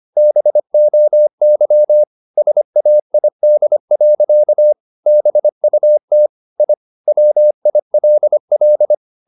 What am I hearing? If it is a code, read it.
Morse code, heard as a keyed tone, about 25 wpm, 600 Hz. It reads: BOY SAID. BUT I WILL
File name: the old man and the sea part 1 25wpm